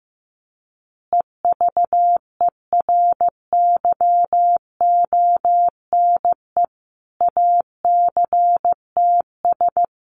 Morse code, heard as a keyed tone, about 15 wpm, 700 Hz. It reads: EVERYONE ACTS
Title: Easy Phrases 15wpm